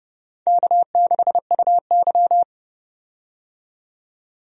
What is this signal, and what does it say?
Morse code, heard as a keyed tone, about 31 wpm, 700 Hz. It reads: K6UY
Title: USA Call Signs - 4 Characters - 30wpm